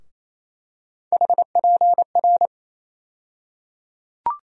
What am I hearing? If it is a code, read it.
Morse code, heard as a keyed tone, about 28 wpm, 700 Hz. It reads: HPR
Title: USA Call Sign Postfixes - 3 Letters 28wpm